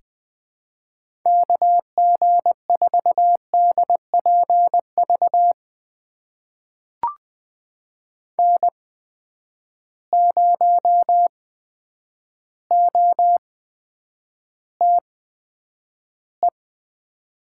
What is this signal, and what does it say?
Morse code, heard as a keyed tone, about 20 wpm, spaced as for 9 wpm, 700 Hz. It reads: KG4DPV N0OTE
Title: USA Call Signs - 8x Character Spacing - 20wpm